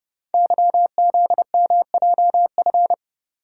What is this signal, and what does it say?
Morse code, heard as a keyed tone, about 30 wpm, 700 Hz. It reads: YZMJF
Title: Code Groups - 5 Letters - 30wpm